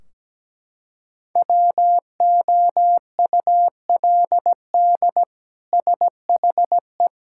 Morse code, heard as a keyed tone, about 17 wpm, 700 Hz. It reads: WOULD SHE
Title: Sentences from Top 1600 Words 17wpm